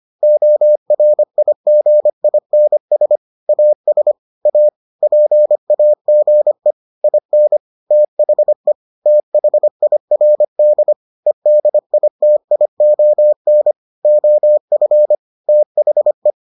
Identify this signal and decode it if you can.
Morse code, heard as a keyed tone, about 25 wpm, 600 Hz. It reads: ORIGINS AS A PAGE IN THE THIRD EDITION OF THE